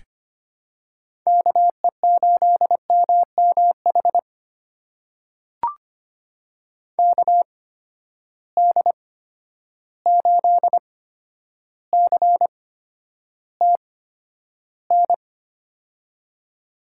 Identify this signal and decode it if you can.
Morse code, heard as a keyed tone, about 25 wpm, spaced as for 11 wpm, 700 Hz. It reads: KE8MMH KD8CTN
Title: USA Call Signs - 6 Characters - 8x Character Spacing - 25wpm